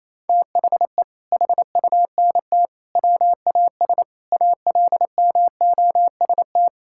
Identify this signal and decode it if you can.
Morse code, heard as a keyed tone, about 28 wpm, 700 Hz. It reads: THE HUNT WAS ALMOST
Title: Sentences from Top 1600 Words 28wpm